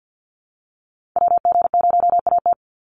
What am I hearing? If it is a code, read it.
Morse code, heard as a keyed tone, about 50 wpm, 700 Hz. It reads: WG0AT
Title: NA SOTA Activator Callsigns 50wpm